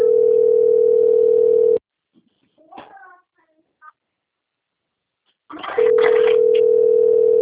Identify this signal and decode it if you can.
Ringback tone (North America), 440 + 480 Hz.